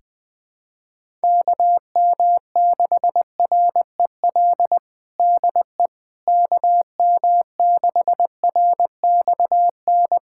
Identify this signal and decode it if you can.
Morse code, heard as a keyed tone, about 20 wpm, 700 Hz. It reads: KM6REL DE KM6RXN
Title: USA Callsign DE Callsign - 6 Characters - 20wpm